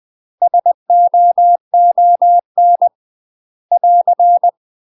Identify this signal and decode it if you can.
Morse code, heard as a keyed tone, about 20 wpm, 700 Hz. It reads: SOON <AR>
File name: UK MASTER QSOs TRAINING/20WPM